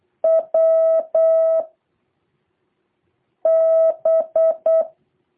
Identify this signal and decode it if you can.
Morse code, heard as a keyed tone, about 8 wpm, 650 Hz. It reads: W B
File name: jh8cbh / 2023cw